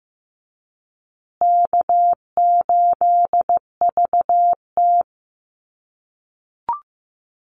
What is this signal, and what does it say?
Morse code, heard as a keyed tone, about 15 wpm, 700 Hz. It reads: K8VT
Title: USA Call Signs Repeated 2x 15wpm